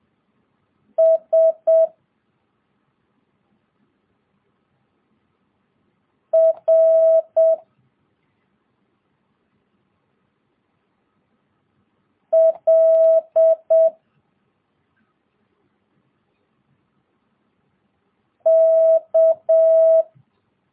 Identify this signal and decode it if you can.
Morse code, heard as a keyed tone, about 7 wpm, 650 Hz. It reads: S R L K
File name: jh8cbh / 2022cw